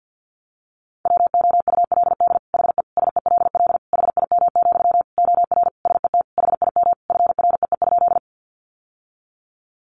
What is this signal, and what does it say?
Morse code, heard as a keyed tone, about 50 wpm, 700 Hz. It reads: WOULD HE SELL HIM, OR SET HIM FREE?
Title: Sentences from Top 1100 Words 50wpm